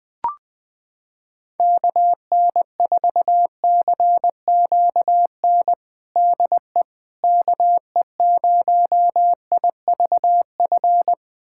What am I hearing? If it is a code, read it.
Morse code, heard as a keyed tone, about 20 wpm, 700 Hz. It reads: KN4CQN DE KE0IVF
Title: USA Callsign DE Callsign - 6 Characters - 20wpm